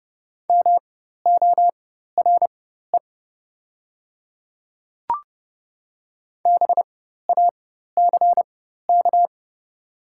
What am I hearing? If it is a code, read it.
Morse code, heard as a keyed tone, about 30 wpm, spaced as for 14 wpm, 700 Hz. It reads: MORE BACK
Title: 4x Character Spacing - Top 200 Words - 30wpm